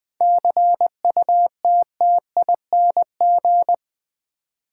Morse code, heard as a keyed tone, about 20 wpm, 700 Hz. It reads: CUTTING
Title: 7 Letter Words - First 3 Letters Given - 20wpm